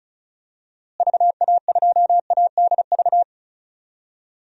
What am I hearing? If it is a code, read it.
Morse code, heard as a keyed tone, about 34 wpm, 700 Hz. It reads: VA2ADV
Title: DXCC-100 Call Signs - 6 Characters - Repeated 3x - 35wpm